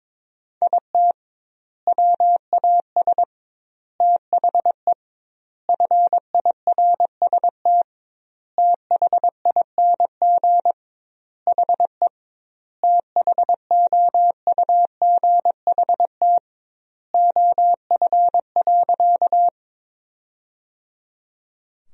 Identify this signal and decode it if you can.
Morse code, heard as a keyed tone, about 22 wpm, 700 Hz. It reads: IT WAS THE FIRST THING HE THOUGHT OF.
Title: 2x Word-Spacing - Sentences from Top 300 Words - 22wpm